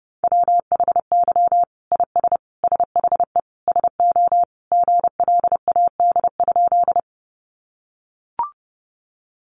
Morse code, heard as a keyed tone, about 28 wpm, 700 Hz. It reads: WHY IS SHE SO GLAD?
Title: Sentences from Top 800 Words 30wpm